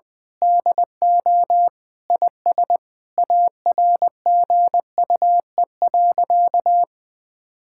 Morse code, heard as a keyed tone, about 20 wpm, 700 Hz. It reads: DO IS ARGUE.
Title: Sentences with 7-Word Phrases - 20wpm